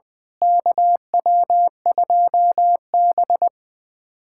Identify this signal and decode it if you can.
Morse code, heard as a keyed tone, about 21 wpm, 700 Hz. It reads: KW2B
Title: USA Callsign DE Callsign - 4 Characters - 20wpm